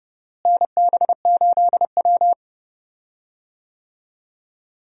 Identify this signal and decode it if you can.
Morse code, heard as a keyed tone, about 30 wpm, 700 Hz. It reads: NB8W
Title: USA Call Signs - 4 Characters - Repeated 2x - 30wpm